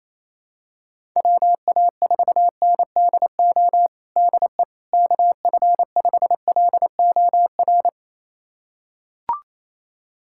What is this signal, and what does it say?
Morse code, heard as a keyed tone, about 27 wpm, 700 Hz. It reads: WA4NDO DE KF5LOR
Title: USA Callsign DE Callsign 28wpm